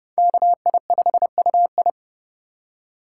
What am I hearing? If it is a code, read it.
Morse code, heard as a keyed tone, about 28 wpm, 700 Hz. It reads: KI5UI